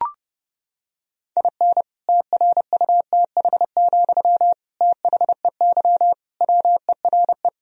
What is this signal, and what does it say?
Morse code, heard as a keyed tone, about 30 wpm, 700 Hz. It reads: IN TRUTH, THEY WERE
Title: Sentences from Top 1000 Words - Review Part 1 - 30wpm